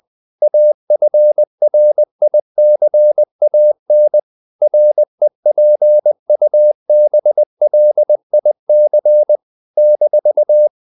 Morse code, heard as a keyed tone, about 20 wpm, 600 Hz. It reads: AFRICAN REPUBLIC -
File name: CW / Practice / NationalCapitals20